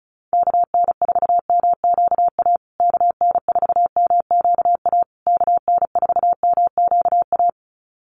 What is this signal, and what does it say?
Morse code, heard as a keyed tone, about 34 wpm, 700 Hz. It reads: KN4MQA KN4MQA KN4MQA
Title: USA Call Signs Repeated 3x 35wpm